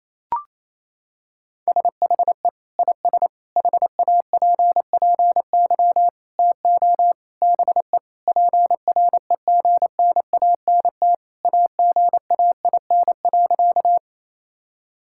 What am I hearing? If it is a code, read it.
Morse code, heard as a keyed tone, about 28 wpm, 700 Hz. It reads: SHE IS HAPPY TO BE PREGNANT AGAIN.
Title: Sentences from Top 1000 Words - Review Part 6 - 28wpm